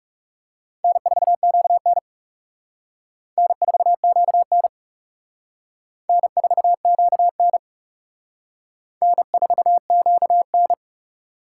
Morse code, heard as a keyed tone, about 35 wpm, 700 Hz. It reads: N4QN N4QN N4QN N4QN
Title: Speed Racing - USA 1x2 Call Signs 30wpm